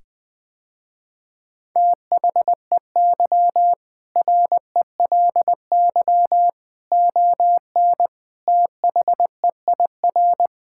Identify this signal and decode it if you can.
Morse code, heard as a keyed tone, about 20 wpm, 700 Hz. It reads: THEY RELY ON THEIR
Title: Sentences with 2-Word Phrases - Spoken Beginning - 20wpm